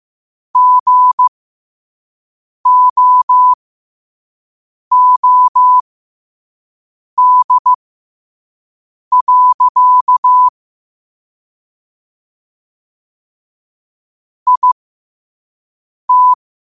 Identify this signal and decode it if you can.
Morse code, heard as a keyed tone, about 15 wpm, spaced as for 5 wpm, 1000 Hz. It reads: GOOD. IT
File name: qotdfiles / 5wpm